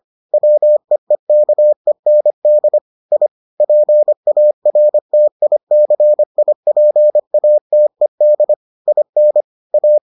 Morse code, heard as a keyed tone, about 25 wpm, 600 Hz. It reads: WEEKEND I PARTICIPATED IN A